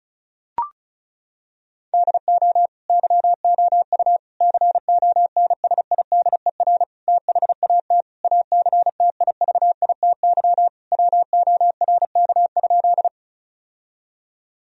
Morse code, heard as a keyed tone, about 36 wpm, 700 Hz. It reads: DO YOU CONSIDER THAT ACTIVITY WORK?
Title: Sentences from Top 1000 Words - Review Part 4 - 35wpm